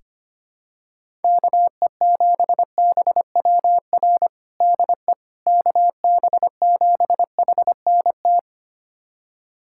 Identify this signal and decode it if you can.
Morse code, heard as a keyed tone, about 25 wpm, 700 Hz. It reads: KE7BWR DE KB7HNT
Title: USA Callsign DE Callsign - 6 Characters - 25wpm